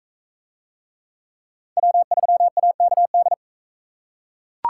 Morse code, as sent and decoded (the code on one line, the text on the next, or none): .-- ...-- .- -.- -..
W3AKD